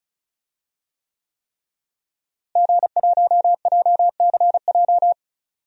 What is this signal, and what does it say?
Morse code, heard as a keyed tone, about 34 wpm, 700 Hz. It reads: G1JCJ
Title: DXCC-100 Call Signs - 5 Characters - Repeated 3x - 35wpm